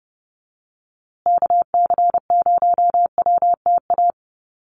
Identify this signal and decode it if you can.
Morse code, heard as a keyed tone, about 29 wpm, 700 Hz. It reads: KC0WTA
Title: USA Call Signs Repeated 3x 30wpm